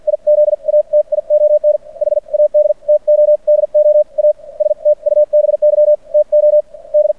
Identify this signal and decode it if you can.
Morse code, heard as a keyed tone, about 50 wpm, spaced as for 42 wpm, 590 Hz. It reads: IZATION HAD TO DO A STUBY TO D
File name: archived / CRSnet